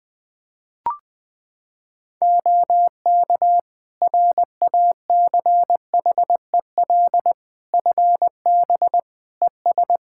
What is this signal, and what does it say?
Morse code, heard as a keyed tone, about 20 wpm, 700 Hz. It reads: OK RACHEL FB ES
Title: Practice QSO - Exchange 2 - Info - 20wpm